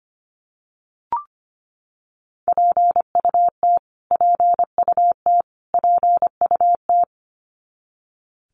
Morse code, heard as a keyed tone, about 25 wpm, 700 Hz. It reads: PUT PUT PUT
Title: Top 200 Words Repeated 25wpm